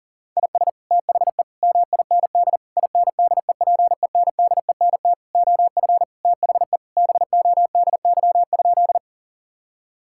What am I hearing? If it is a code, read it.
Morse code, heard as a keyed tone, about 40 wpm, 700 Hz. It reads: IS THE MIND INDEPENDENT OF THE BODY?
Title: Sentences from Top 1000 Words - Review Part 6 - 40wpm